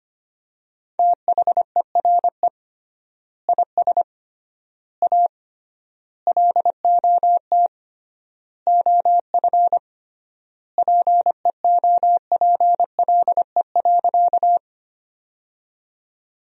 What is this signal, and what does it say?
Morse code, heard as a keyed tone, about 25 wpm, 700 Hz. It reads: THERE IS A LOT OF PEOPLE.
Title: 3x Word-Spacing - Sentences from Top 500 Words - 25wpm